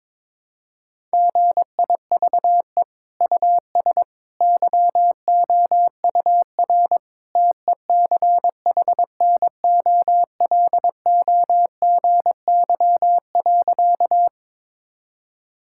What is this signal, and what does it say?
Morse code, heard as a keyed tone, about 21 wpm, 700 Hz. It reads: GIVE US YOUR TECHNOLOGY.
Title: Sentences from Top 1000 Words - Review Part 5 - 22wpm